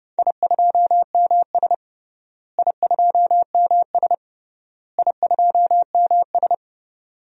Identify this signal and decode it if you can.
Morse code, heard as a keyed tone, about 32 wpm, 700 Hz. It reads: I2MS I2MS I2MS
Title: DXCC-100 Call Signs - 4 Characters - Repeated 3x - 30wpm